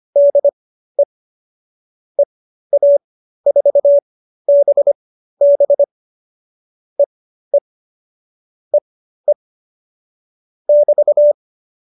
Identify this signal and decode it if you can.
Morse code, heard as a keyed tone, about 25 wpm, spaced as for 13 wpm, 570 Hz. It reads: DE EA4BB EE EE <BT>